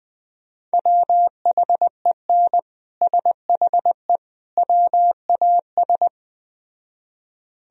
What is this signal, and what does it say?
Morse code, heard as a keyed tone, about 19 wpm, 700 Hz. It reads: WHEN SHE WAS
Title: Speed-Racing - Sets of 3 Words 17wpm